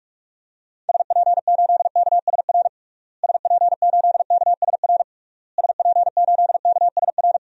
Morse code, heard as a keyed tone, about 41 wpm, 700 Hz. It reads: SP8KSR SP8KSR SP8KSR
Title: DXCC-100 Call Signs - 6 Characters - Repeated 3x - 45wpm